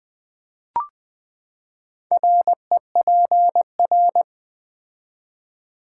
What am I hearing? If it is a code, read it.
Morse code, heard as a keyed tone, about 20 wpm, 700 Hz. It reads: REPR
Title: Build A Word - 4 Letters At A Time - 12-Letter Words - 20wpm